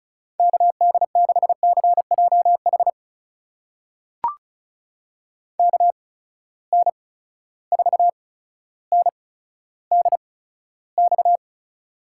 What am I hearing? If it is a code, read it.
Morse code, heard as a keyed tone, about 36 wpm, spaced as for 15 wpm, 700 Hz. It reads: KD6CJH KN4NDX
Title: USA Call Signs - 8x Character Spacing - 35wpm